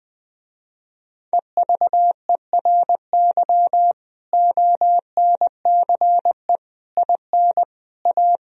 Morse code, heard as a keyed tone, about 20 wpm, 700 Hz. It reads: EVERY ONCE IN A
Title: Speed-Racing - Sets of 5 Words 17wpm